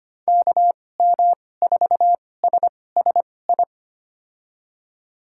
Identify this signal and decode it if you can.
Morse code, heard as a keyed tone, about 25 wpm, spaced as for 18 wpm, 700 Hz. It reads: KM4SSI
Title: USA Call Signs - 6 Characters - 2x Character Spacing - 25wpm